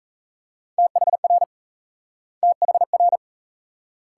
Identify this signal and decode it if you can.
Morse code, heard as a keyed tone, about 42 wpm, 700 Hz. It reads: THR THR